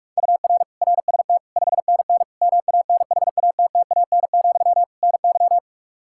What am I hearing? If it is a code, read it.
Morse code, heard as a keyed tone, about 46 wpm, 700 Hz. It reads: UR RST 5NN MANHATTAN, NY
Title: Practice QSO - Exchange 1 - Beginning - 45wpm